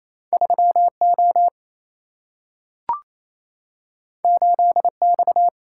Morse code, heard as a keyed tone, about 28 wpm, 700 Hz. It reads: 3O 8X
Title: Single-Letter Single-Number 28wpm